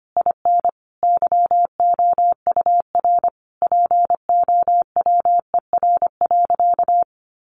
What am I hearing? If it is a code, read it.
Morse code, heard as a keyed tone, about 25 wpm, 700 Hz. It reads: IN YOUR POWER.
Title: Sentences from Top 600 Words 25wpm